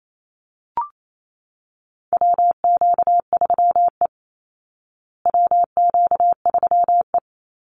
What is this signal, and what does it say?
Morse code, heard as a keyed tone, about 28 wpm, 690 Hz. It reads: WQ3E WQ3E
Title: Call Signs Repeated 3x - 28wpm